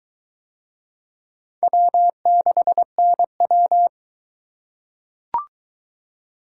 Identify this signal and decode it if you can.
Morse code, heard as a keyed tone, about 23 wpm, 700 Hz. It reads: W6NW